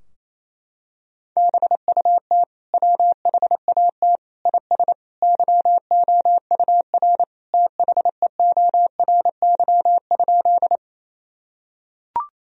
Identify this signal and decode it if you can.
Morse code, heard as a keyed tone, about 28 wpm, 700 Hz. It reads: BUT WHAT IS YOUR THEORY?
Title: Sentences from Top 1000 Words - Review Part 3 - 28wpm